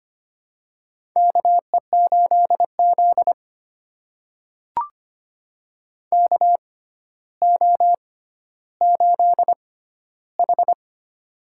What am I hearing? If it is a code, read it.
Morse code, heard as a keyed tone, about 25 wpm, spaced as for 13 wpm, 700 Hz. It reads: KE8Z KO8H
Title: USA Call Signs - 4 Characters - 6x Character Spacing - 25wpm